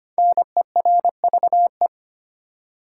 Morse code, heard as a keyed tone, about 25 wpm, 700 Hz. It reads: NERVE
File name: 5 Letter Words - First 3 Letters Given - 25wpm